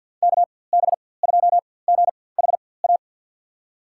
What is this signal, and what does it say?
Morse code, heard as a keyed tone, about 53 wpm, spaced as for 24 wpm, 700 Hz. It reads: KB2GHA
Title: USA Call Signs - 6 Characters - 4x Character Spacing - 50wpm